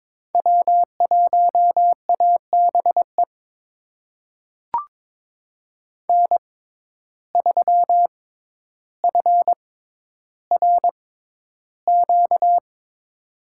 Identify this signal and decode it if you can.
Morse code, heard as a keyed tone, about 22 wpm, spaced as for 11 wpm, 700 Hz. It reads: W1ABE N3FRQ
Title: USA Call Signs - 6x Character Spacing - 22wpm